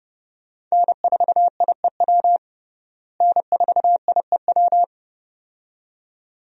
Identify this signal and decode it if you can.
Morse code, heard as a keyed tone, about 30 wpm, 700 Hz. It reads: N4IEW N4IEW
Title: USA Call Signs - 5 Characters - Repeated 2x - 30wpm